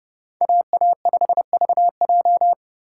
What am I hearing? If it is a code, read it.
Morse code, heard as a keyed tone, about 38 wpm, 700 Hz. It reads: AA5VJ